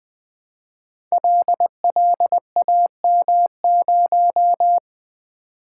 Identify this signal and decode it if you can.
Morse code, heard as a keyed tone, about 20 wpm, 700 Hz. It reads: LLAM0
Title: Code Groups - 5 Letter:Numbers - 20wpm